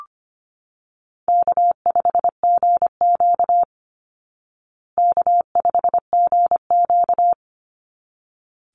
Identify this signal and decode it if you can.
Morse code, heard as a keyed tone, about 25 wpm, 700 Hz. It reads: K5GQ K5GQ
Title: Easy Call Signs Repeated 2x - 25wpm